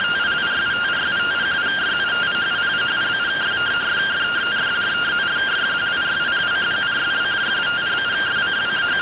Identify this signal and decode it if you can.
Two-tone teleprinter signal, about 170 Hz shift.